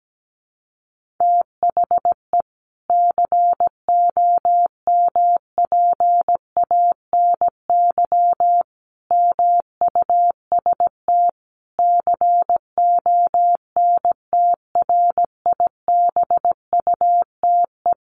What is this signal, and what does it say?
Morse code, heard as a keyed tone, about 17 wpm, 700 Hz. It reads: THE COMPANY MUST CONTRIBUTE
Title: Sentences from Top 1400 Words 17wpm